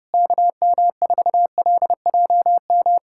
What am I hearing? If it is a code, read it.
Morse code, heard as a keyed tone, about 30 wpm, 700 Hz. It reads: KM4LJM
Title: USA Callsign DE Callsign - 6 Characters - 30wpm